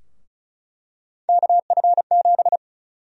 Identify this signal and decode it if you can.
Morse code, heard as a keyed tone, about 36 wpm, 700 Hz. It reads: KF7
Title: USA Call Sign Prefixes - 35wpm